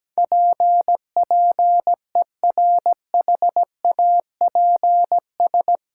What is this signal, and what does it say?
Morse code, heard as a keyed tone, about 17 wpm, 700 Hz. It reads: PPERHAPS
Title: Sentences from Top 1100 Words - New Word Drill - 17wpm